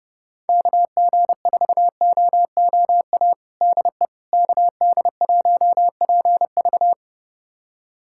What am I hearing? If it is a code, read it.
Morse code, heard as a keyed tone, about 32 wpm, 700 Hz. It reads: KG4OOA DE KD1PV
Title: USA Callsign DE Callsign 30wpm